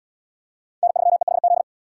Morse code, heard as a keyed tone, about 67 wpm, 700 Hz. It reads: N3EHZ